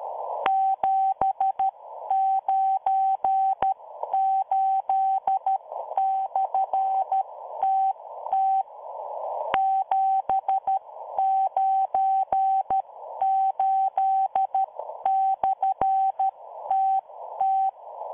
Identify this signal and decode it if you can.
Morse code, heard as a keyed tone, about 13 wpm, spaced as for 11 wpm, 770 Hz. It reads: 798/TT 798/TT